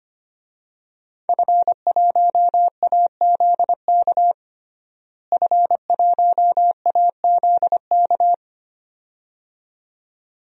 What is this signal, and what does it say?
Morse code, heard as a keyed tone, about 25 wpm, 700 Hz. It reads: F1AZK F1AZK
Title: DXCC-100 Call Signs - 5 Characters - Repeated 2x - 25wpm